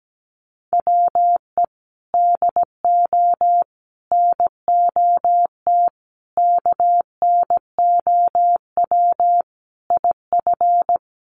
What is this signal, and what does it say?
Morse code, heard as a keyed tone, about 17 wpm, 700 Hz. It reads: WE DO NOT KNOW IF
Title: Sets of 5 Words Encore2 17wpm